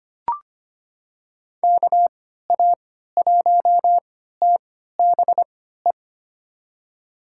Morse code, heard as a keyed tone, about 25 wpm, spaced as for 14 wpm, 700 Hz. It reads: KA1TBE
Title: USA Call Signs - 6 Characters - 3x Character Spacing - 25wpm